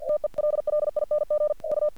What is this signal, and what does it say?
Morse code, heard as a keyed tone, about 48 wpm, 600 Hz. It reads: TELLING U
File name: archived / CRSnet